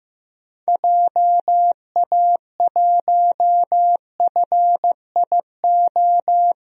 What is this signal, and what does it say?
Morse code, heard as a keyed tone, about 15 wpm, 700 Hz. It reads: JA1FIO
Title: DXCC-100 Call Signs - 6 Characters - 15wpm